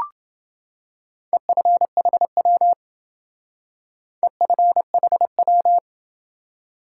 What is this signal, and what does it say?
Morse code, heard as a keyed tone, about 28 wpm, 700 Hz. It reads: EFHW EFHW